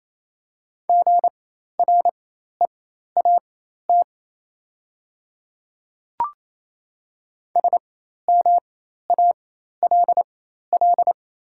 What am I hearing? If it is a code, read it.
Morse code, heard as a keyed tone, about 27 wpm, spaced as for 13 wpm, 690 Hz. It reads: GREAT SMALL
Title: 4x Character Spacing - Top 200 Words - 28wpm